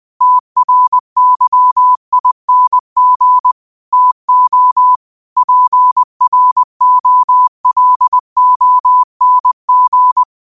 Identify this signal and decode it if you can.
Morse code, heard as a keyed tone, about 20 wpm, 1000 Hz. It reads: TRYING TO PROLONG